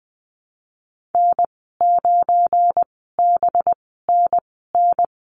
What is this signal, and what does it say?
Morse code, heard as a keyed tone, about 21 wpm, spaced as for 15 wpm, 700 Hz. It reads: N9BNN
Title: 193 Lesson 31g - Letter v - Call Signs 20wpm